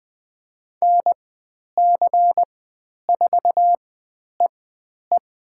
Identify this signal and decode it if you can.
Morse code, heard as a keyed tone, about 20 wpm, spaced as for 10 wpm, 700 Hz. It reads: NC4EE
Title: 109 Lesson 18g - Letter m - Call Signs 20wpm